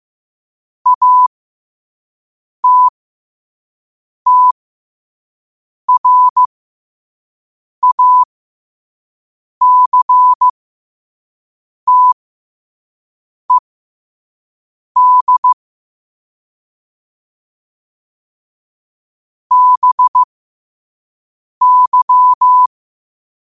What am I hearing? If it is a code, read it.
Morse code, heard as a keyed tone, about 15 wpm, spaced as for 5 wpm, 1000 Hz. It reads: ATTRACTED BY